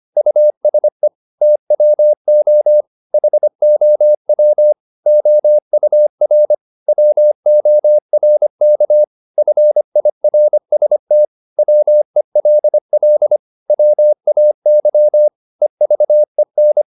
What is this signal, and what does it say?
Morse code, heard as a keyed tone, about 25 wpm, 600 Hz. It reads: USE TWO HOW OUR WORK FIRST WELL WAY EVEN
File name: Top-100-Words-25